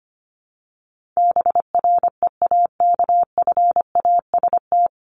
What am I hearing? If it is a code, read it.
Morse code, heard as a keyed tone, about 25 wpm, 700 Hz. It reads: BREAKFAST
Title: Top 1251-1500 Words 25wpm